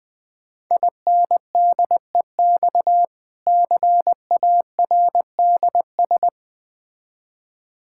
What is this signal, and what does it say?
Morse code, heard as a keyed tone, about 20 wpm, 690 Hz. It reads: INDEX CARDS
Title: Difficult Words with Hints - 20wpm